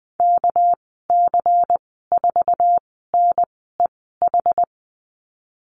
Morse code, heard as a keyed tone, about 20 wpm, spaced as for 14 wpm, 700 Hz. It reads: KC4NEH
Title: 238 Lesson 38g - Letter q - Call Signs 20wpm